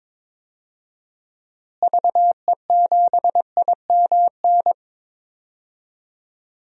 Morse code, heard as a keyed tone, about 22 wpm, 700 Hz. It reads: VE7IMN
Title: DXCC-100 Call Signs - 6 Characters - Repeated 2x - 22wpm